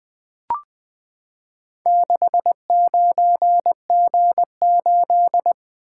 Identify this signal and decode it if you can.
Morse code, heard as a keyed tone, about 20 wpm, 700 Hz. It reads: 69G8
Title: Four Letter-Number Combinations - 20wpm